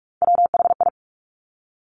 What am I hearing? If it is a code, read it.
Morse code, heard as a keyed tone, about 46 wpm, 700 Hz. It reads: WHI
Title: Top Three Letter Combinations in Words 45wpm